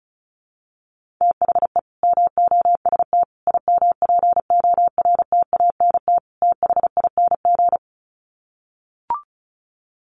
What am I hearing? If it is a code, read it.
Morse code, heard as a keyed tone, about 34 wpm, 700 Hz. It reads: THE MOST IMPORTANT THING
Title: Sets of 4 Words 35wpm